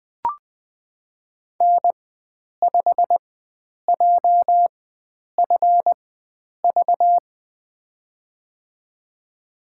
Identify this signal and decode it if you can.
Morse code, heard as a keyed tone, about 20 wpm, spaced as for 9 wpm, 700 Hz. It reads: N5JFV
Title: USA Call Signs - 5 Characters - 4x Character Spacing - 20wpm